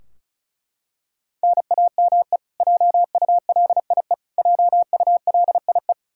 Morse code, heard as a keyed tone, about 36 wpm, 700 Hz. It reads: NAME JULIE JULIE
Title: Practice QSO - Exchange 1 - Beginning - 35wpm